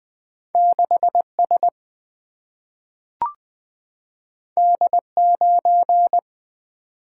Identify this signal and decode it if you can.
Morse code, heard as a keyed tone, about 20 wpm, 700 Hz. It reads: 6 S D 9
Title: Single-Letter Single-Number 20wpm